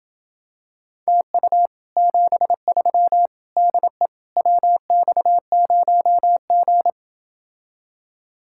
Morse code, heard as a keyed tone, about 28 wpm, 700 Hz. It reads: TU 73 DE WX0G